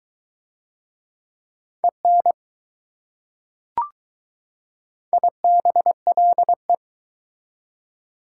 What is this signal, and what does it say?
Morse code, heard as a keyed tone, about 22 wpm, 700 Hz. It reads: EN IBLE